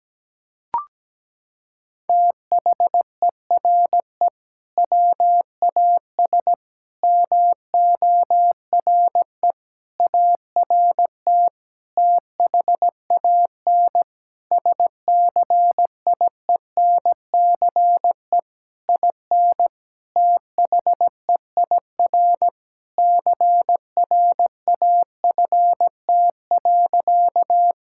Morse code, heard as a keyed tone, about 17 wpm, 700 Hz. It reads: THERE WAS MORE ART THAN SCIENCE IN THEIR CRAFT.
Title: Sentences from Top 1000 Words - Review Part 6 - 17wpm